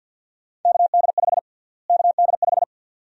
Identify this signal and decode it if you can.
Morse code, heard as a keyed tone, about 47 wpm, 700 Hz. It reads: KD5 KD5
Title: USA Call Sign Prefixes - Repeated 2x - 50wpm